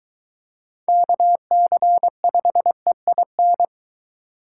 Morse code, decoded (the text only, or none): KC5EIN